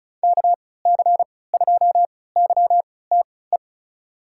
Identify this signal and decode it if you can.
Morse code, heard as a keyed tone, about 36 wpm, spaced as for 20 wpm, 700 Hz. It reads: KC2YTE
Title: USA Call Signs - 6 Characters - 3x Character Spacing - 35wpm